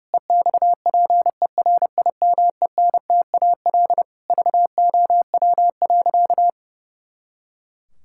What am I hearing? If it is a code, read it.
Morse code, heard as a keyed tone, about 30 wpm, 700 Hz. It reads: EXPERIMENTAL VOW.
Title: Pangram Challenge v2 - 30wpm